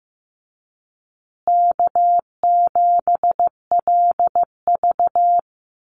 Morse code, heard as a keyed tone, about 15 wpm, 700 Hz. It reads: K7LV
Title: Call Signs Repeated 2x - 15wpm